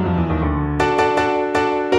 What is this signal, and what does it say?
There is no code.